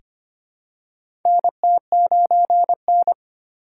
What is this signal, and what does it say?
Morse code, heard as a keyed tone, about 25 wpm, 700 Hz. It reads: NT9N
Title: USA Call Signs - 4 Characters - 2x Character Spacing - 25wpm